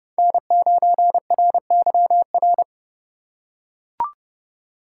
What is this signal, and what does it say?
Morse code, heard as a keyed tone, about 30 wpm, 700 Hz. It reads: N9RYR
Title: USA Call Signs - 5 Characters - 3x Character Spacing - 30wpm